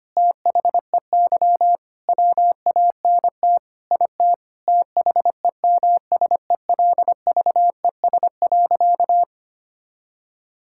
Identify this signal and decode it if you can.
Morse code, heard as a keyed tone, about 25 wpm, 700 Hz. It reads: THEY WANT IT THEMSELVES.
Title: Sentences from Top 1000 Words - Review Part 5 - 25wpm